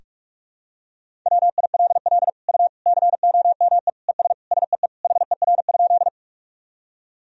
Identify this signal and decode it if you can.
Morse code, heard as a keyed tone, about 46 wpm, 700 Hz. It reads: WILL U COME ES SEE HER?
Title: Sentences from Top 100 Words and Abbreviations 45wpm